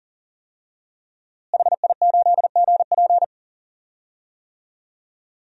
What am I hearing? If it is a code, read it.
Morse code, heard as a keyed tone, about 38 wpm, 700 Hz. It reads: HI8GP
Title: DXCC-100 Call Signs - 5 Characters - Repeated 2x - 40wpm